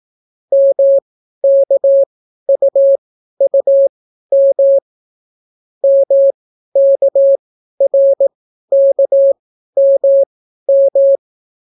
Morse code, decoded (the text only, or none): MKUUM MKRKMM